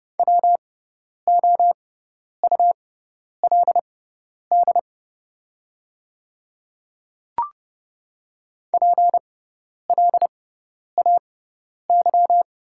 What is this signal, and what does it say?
Morse code, heard as a keyed tone, about 30 wpm, spaced as for 10 wpm, 700 Hz. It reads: WOULD PLAY
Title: 6x Character Spacing - Top 200 Words - 30wpm